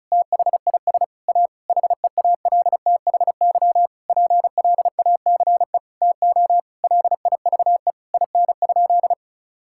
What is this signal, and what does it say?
Morse code, heard as a keyed tone, about 36 wpm, 700 Hz. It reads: THIS A HEALTHY PLACE TO LIVE IN?
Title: Sentences from Top 1000 Words - Review Part 5 - 35wpm